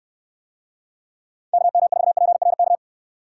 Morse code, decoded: KM4PRK

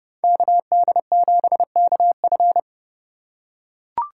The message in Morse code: KD7KF

